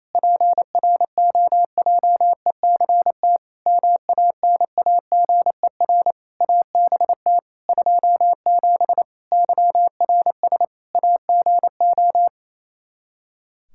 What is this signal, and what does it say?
Morse code, heard as a keyed tone, about 27 wpm, 700 Hz. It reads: PROJECT MANAGER ABT 27 YRS AGO
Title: Practice QSO - Exchange 3 - More Info - 28wpm